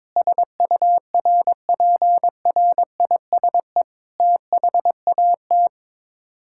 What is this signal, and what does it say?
Morse code, heard as a keyed tone, about 22 wpm, 700 Hz. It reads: SURPRISE THAT
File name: practice / 22wpm / Sets of 8 Words - 22wpm